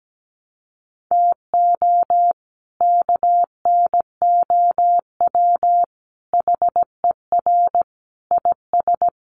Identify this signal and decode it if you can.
Morse code, heard as a keyed tone, about 17 wpm, 700 Hz. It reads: TO KNOW HER IS
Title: Sentences from Top 400 Words 17wpm